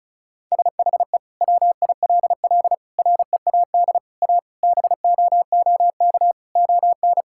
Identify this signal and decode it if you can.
Morse code, heard as a keyed tone, about 34 wpm, 700 Hz. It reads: SHE WILL READ A BOOK ON
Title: Sentences from Top 1000 Words - Review Part 6 - 35wpm